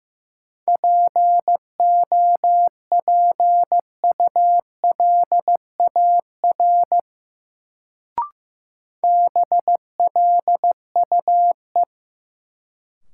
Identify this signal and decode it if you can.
Morse code, heard as a keyed tone, about 15 wpm, 700 Hz. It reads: POPULAR BLUE
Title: Sentences from Top 1200 Words - New Word Drill Alt - 15wpm